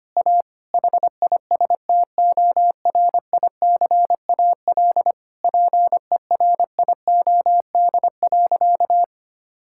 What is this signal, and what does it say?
Morse code, heard as a keyed tone, about 25 wpm, 700 Hz. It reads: A HISTORICAL PERIOD.
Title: Sentences from Top 1000 Words - Review Part 5 - 25wpm